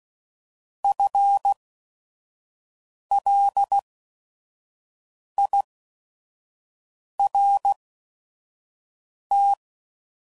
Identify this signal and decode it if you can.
Morse code, heard as a keyed tone, about 16 wpm, spaced as for 5 wpm, 800 Hz. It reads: FLIRT
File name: Prefixes / By First Letter / Fs